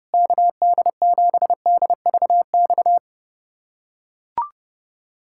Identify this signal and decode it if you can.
Morse code, heard as a keyed tone, about 32 wpm, 700 Hz. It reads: KD7DVX
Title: USA Callsign DE Callsign - 6 Characters - 30wpm